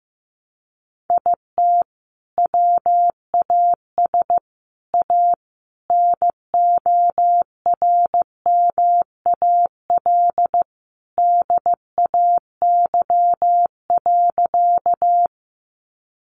Morse code, heard as a keyed tone, about 15 wpm, 700 Hz. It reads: IT WAS A NORMAL DAY.
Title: Sentences from Top 1200 Words 15wpm